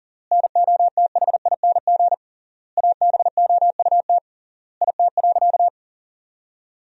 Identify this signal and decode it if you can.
Morse code, heard as a keyed tone, about 40 wpm, 700 Hz. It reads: NOTHING ABOUT IT.
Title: 3x Word-Spacing - Sentences from Top 500 Words - 40wpm